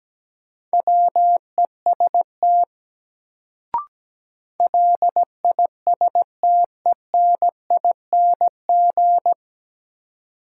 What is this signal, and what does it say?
Morse code, heard as a keyed tone, about 17 wpm, 700 Hz. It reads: WEST LISTENING
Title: Sets of 5 Words - New Word Drill - 17wpm